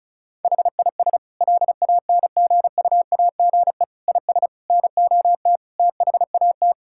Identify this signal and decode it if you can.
Morse code, heard as a keyed tone, about 36 wpm, 700 Hz. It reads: HIS LANGUAGE IS NOT THAT
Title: Sentences from Top 1600 Words 35wpm